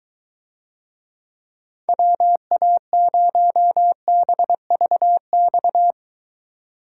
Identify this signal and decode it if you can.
Morse code, heard as a keyed tone, about 22 wpm, 700 Hz. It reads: WA0BVX